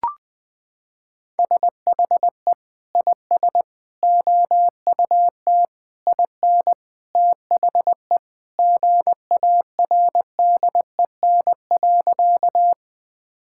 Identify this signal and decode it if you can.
Morse code, heard as a keyed tone, about 20 wpm, 700 Hz. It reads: SHE IS OUT IN THE GARDEN.
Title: Sentences with 3-Word Phrases - 20wpm